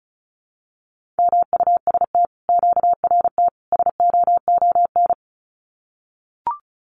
Morse code, heard as a keyed tone, about 36 wpm, 700 Hz. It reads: MUST QRT SOON
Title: Common QSO Phrases Encore 35wpm